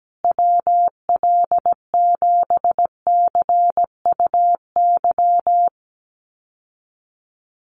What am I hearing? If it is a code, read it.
Morse code, heard as a keyed tone, about 17 wpm, 700 Hz. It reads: WL7CUY
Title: USA Call Signs Repeated 2x 17wpm